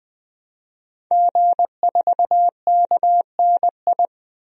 Morse code, heard as a keyed tone, about 21 wpm, 700 Hz. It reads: G4KNI
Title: DXCC-100 Call Signs - 5 Characters - 20wpm